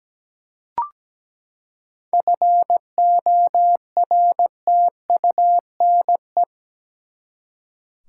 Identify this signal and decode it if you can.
Morse code, heard as a keyed tone, about 17 wpm, 700 Hz. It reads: FORTUNE
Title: Seven Letter Words - 17wpm